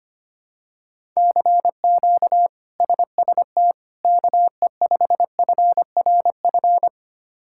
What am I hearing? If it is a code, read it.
Morse code, heard as a keyed tone, about 25 wpm, 700 Hz. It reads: CQ SST KE5FRF
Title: K1USN SST QSO Practice - 25wpm